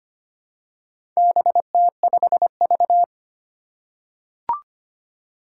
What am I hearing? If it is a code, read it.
Morse code, heard as a keyed tone, about 25 wpm, 700 Hz. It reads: BT5V
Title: Four Challenging Letter-Number Combinations 25wpm